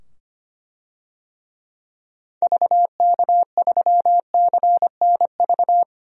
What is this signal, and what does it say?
Morse code, heard as a keyed tone, about 25 wpm, 700 Hz. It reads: VK3CNV
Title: DXCC-100 Call Signs - 6 Characters - 25wpm